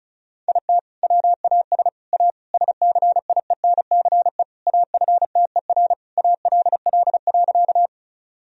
Morse code, heard as a keyed tone, about 36 wpm, 700 Hz. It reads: IT WAS A SCIENCE AFTER ALL.
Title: Sentences from Top 1000 Words - Review Part 2 - 35wpm